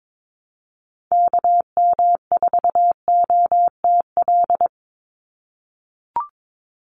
Morse code, heard as a keyed tone, about 22 wpm, 700 Hz. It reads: KM4OTL
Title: USA Call Signs 22wpm